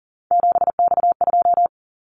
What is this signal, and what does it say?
Morse code, heard as a keyed tone, about 40 wpm, 700 Hz. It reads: 7X2
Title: Callsign Prefixes for Top 100 DXCC - 40wpm